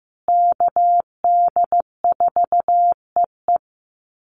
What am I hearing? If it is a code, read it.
Morse code, heard as a keyed tone, about 15 wpm, 700 Hz. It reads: KD4EE
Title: Call Signs Repeated 3x - 15wpm